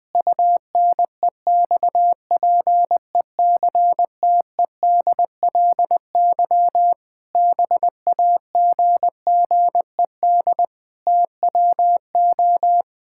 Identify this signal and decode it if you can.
Morse code, heard as a keyed tone, about 20 wpm, 700 Hz. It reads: UNEXPECTEDLY BAGGED TWO
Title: Pangram Challenge v2 - 20wpm